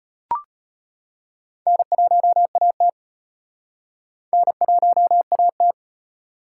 Morse code, heard as a keyed tone, about 38 wpm, 700 Hz. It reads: N1AT N1AT